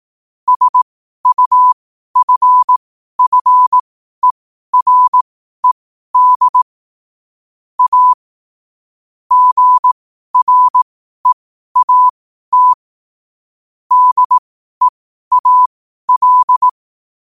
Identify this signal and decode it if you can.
Morse code, heard as a keyed tone, about 18 wpm, spaced as for 12 wpm, 1000 Hz. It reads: SUFFERED A GREAT DEAL
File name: qotdfiles / 12wpm